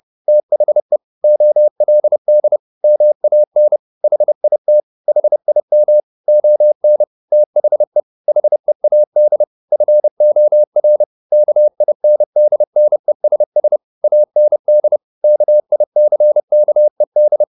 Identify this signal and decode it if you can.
Morse code, heard as a keyed tone, about 30 wpm, 600 Hz. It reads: THE OLD MAN HIT HIM ON THE HEAD FOR KINDNESS AND KICKED
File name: the old man and the sea / the old man and the sea 1 30wpm